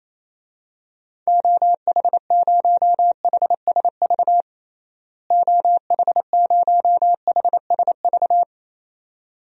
Morse code, heard as a keyed tone, about 28 wpm, 700 Hz. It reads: OH0HSV OH0HSV
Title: DXCC-100 Call Signs - 6 Characters - Repeated 2x - 28wpm